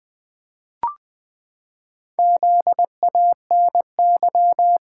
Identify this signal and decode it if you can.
Morse code, heard as a keyed tone, about 20 wpm, 700 Hz. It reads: ZANY
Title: Pangram Challenge v2 - 20wpm